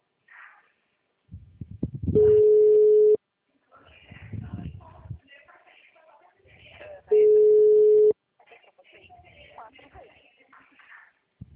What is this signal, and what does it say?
Ringback tone (Europe, 425 Hz), with speech.
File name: out- -2016